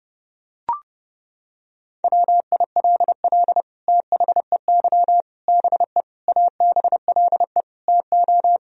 Morse code, heard as a keyed tone, about 30 wpm, 700 Hz. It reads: WILL THEY BE ABLE TO
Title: Sentences from Top 1000 Words - Review Part 3 - 30wpm